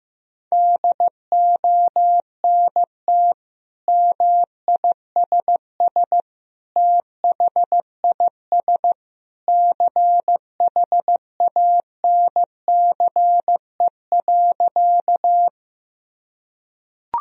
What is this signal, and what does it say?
Morse code, heard as a keyed tone, about 15 wpm, 700 Hz. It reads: DONT MISS THIS CHANCE.
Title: Sentences from Top 1000 Words - Review Part 3 - 15wpm